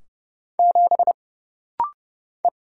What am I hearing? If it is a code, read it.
Morse code, heard as a keyed tone, about 30 wpm, 700 Hz. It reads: 7 E
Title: Single Letter-Number Alt - Rapid-Fire 30wpm